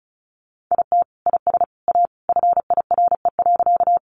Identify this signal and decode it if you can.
Morse code, heard as a keyed tone, about 34 wpm, 700 Hz. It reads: IT IS A FIRE.
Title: Sentences from Top 400 Words 35wpm